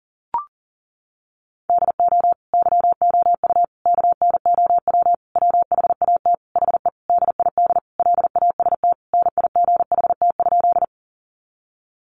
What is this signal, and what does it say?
Morse code, heard as a keyed tone, about 37 wpm, 700 Hz. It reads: DO YOU KNOW WHAT HE DID LAST NIGHT?
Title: Sentences from Top 300 Words 40wpm